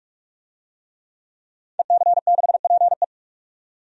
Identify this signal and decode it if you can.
Morse code, heard as a keyed tone, about 46 wpm, 700 Hz. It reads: EC6PE
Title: DXCC-100 Call Signs - 5 Characters - 45wpm